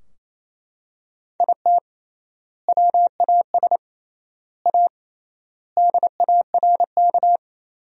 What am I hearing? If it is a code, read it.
Morse code, heard as a keyed tone, about 27 wpm, 700 Hz. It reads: IT WAS A DARK
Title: 3x Word-Spacing - Sentences from Top 500 Words - 28wpm